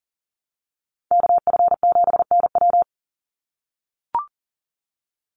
Morse code, heard as a keyed tone, about 40 wpm, 700 Hz. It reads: KF7NW
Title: USA Call Signs 40wpm